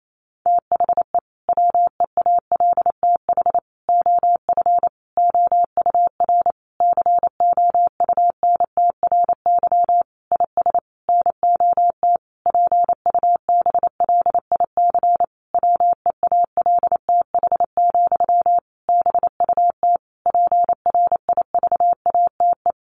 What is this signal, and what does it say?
Morse code, heard as a keyed tone, about 28 wpm, 700 Hz. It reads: THE WEALTH OF OUR COUNTRY IS NOT PUBLIC WEALTH, BUT PRIVATE
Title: Sentences from Top 1200 Words 28wpm